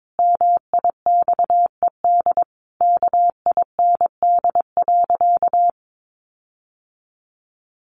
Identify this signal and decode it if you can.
Morse code, heard as a keyed tone, about 22 wpm, 700 Hz. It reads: MIXED KIND.
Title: Sentences from Top 1500 Words 22wpm